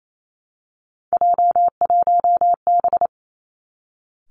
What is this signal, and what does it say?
Morse code, heard as a keyed tone, about 28 wpm, 700 Hz. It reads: J1B
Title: Three Challenging Letter-Number Combinations 28wpm